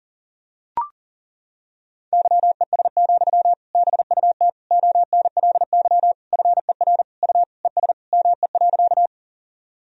Morse code, heard as a keyed tone, about 40 wpm, 700 Hz. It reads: YES, BUT ONLY FER U ES ME.
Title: Sentences from Top 500 Words and Abbreviations 40wpm